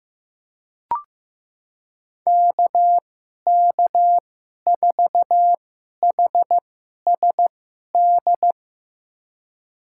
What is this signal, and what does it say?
Morse code, heard as a keyed tone, about 15 wpm, spaced as for 11 wpm, 700 Hz. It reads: KK4HSD
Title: USA Call Signs - 6 Characters - 2x Character Spacing - 15wpm